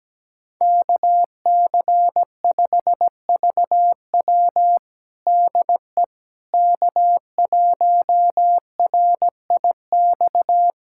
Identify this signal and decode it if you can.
Morse code, heard as a keyed tone, about 17 wpm, 700 Hz. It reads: KC5VW DE K1RIX
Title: USA Callsign DE Callsign 17wpm